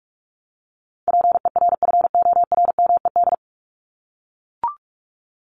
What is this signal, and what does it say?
Morse code, heard as a keyed tone, about 44 wpm, 700 Hz. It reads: PERFORMED
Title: ED Suffix - Top 100 Words with Base Words 45wpm